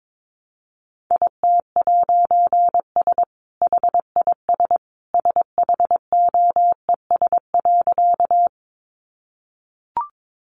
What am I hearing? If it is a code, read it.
Morse code, heard as a keyed tone, about 22 wpm, 700 Hz. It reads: IT'S HIS SHOES.
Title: A Sound Of Thunder 22wpm